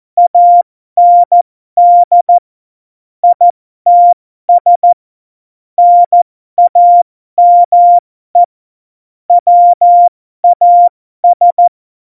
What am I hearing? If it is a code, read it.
Morse code, heard as a keyed tone, about 14 wpm, spaced as for 12 wpm, 700 Hz. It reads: AND ITS NAME WAS
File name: Morsekurs / Witze